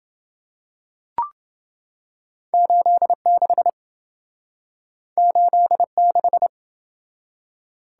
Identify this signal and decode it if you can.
Morse code, heard as a keyed tone, about 27 wpm, 700 Hz. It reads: 86 86